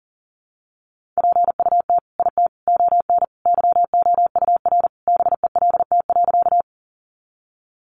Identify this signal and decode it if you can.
Morse code, heard as a keyed tone, about 40 wpm, 700 Hz. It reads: PUT IT ON YOUR BELT.
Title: Sentences from Top 1400 Words 40wpm